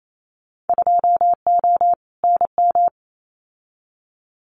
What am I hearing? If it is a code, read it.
Morse code, heard as a keyed tone, about 28 wpm, 700 Hz. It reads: 2O NM
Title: Winter Field Day Exchanges 28wpm